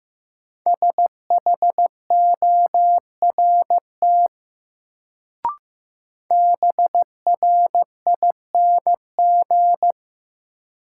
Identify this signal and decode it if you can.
Morse code, heard as a keyed tone, about 15 wpm, 700 Hz. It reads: SHORT BRING
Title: Sets of 5 Words Encore - New Word Drill - 15wpm